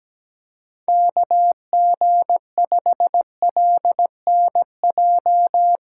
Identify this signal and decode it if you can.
Morse code, heard as a keyed tone, about 18 wpm, 700 Hz. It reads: KG5LNJ
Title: Speed Racing - USA Call Signs 17wpm